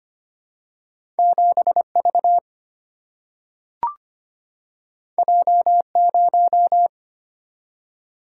Morse code, heard as a keyed tone, about 25 wpm, 700 Hz. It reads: 7V J0
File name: Two Letter-Number Combinations - 25wpm